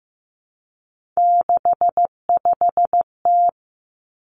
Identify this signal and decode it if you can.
Morse code, heard as a keyed tone, about 15 wpm, 700 Hz. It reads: 65T
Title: Three Challenging Letter-Number Combinations 15wpm